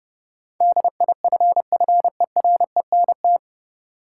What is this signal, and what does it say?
Morse code, heard as a keyed tone, about 32 wpm, 700 Hz. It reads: DIFFERENT
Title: Sentences from Top 600 Words - New Word Drill - 30wpm